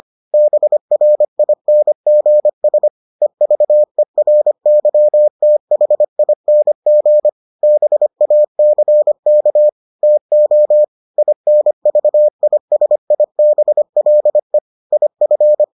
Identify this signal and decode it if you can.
Morse code, heard as a keyed tone, about 25 wpm, 600 Hz. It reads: BRINGS EVERYTHING BACK TO INVISIBLE IF